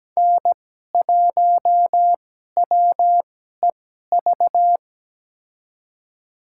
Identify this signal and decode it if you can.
Morse code, heard as a keyed tone, about 17 wpm, spaced as for 12 wpm, 700 Hz. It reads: N1WEV
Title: USA Call Signs - 5 Characters - 2x Character Spacing - 17wpm